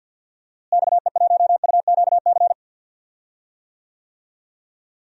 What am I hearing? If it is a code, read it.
Morse code, heard as a keyed tone, about 48 wpm, 700 Hz. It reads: XE1UQC
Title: DXCC-100 Call Signs - 6 Characters - Repeated 3x - 50wpm